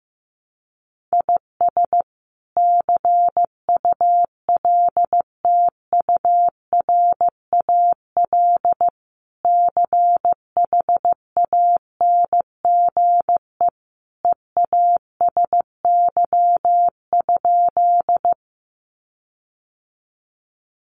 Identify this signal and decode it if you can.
Morse code, heard as a keyed tone, about 15 wpm, 700 Hz. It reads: IS CULTURAL CHANGE EASY?
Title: Sentences from Top 1500 Words 15wpm